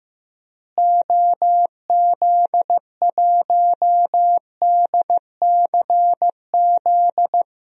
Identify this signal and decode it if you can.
Morse code, heard as a keyed tone, about 16 wpm, 700 Hz. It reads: OZ1DCZ
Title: DXCC-100 Call Signs - 6 Characters - Repeated 2x - 15wpm